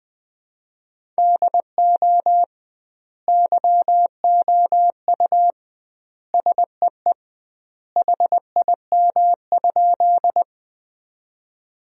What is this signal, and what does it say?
Morse code, heard as a keyed tone, about 20 wpm, 700 Hz. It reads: DO YOU SEE HIM?
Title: 2x Word-Spacing - Sentences from Top 100 Words - 20wpm